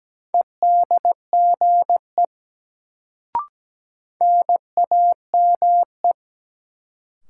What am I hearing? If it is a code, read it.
Morse code, heard as a keyed tone, about 17 wpm, 700 Hz. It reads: EDGE NAME
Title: Sets of 3 Words - New Word Drill - 17wpm